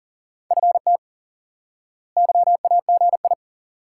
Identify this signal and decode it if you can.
Morse code, heard as a keyed tone, about 37 wpm, 700 Hz. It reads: FT YAGI
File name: IQE Course 3 - Lesson 89 - YAGI - Two Elements - 40wpm